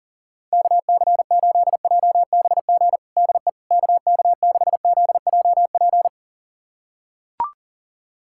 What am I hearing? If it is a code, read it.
Morse code, heard as a keyed tone, about 40 wpm, 700 Hz. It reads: KC8JBG DE KK6ZJP
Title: USA Callsign DE Callsign - 6 Characters - 40wpm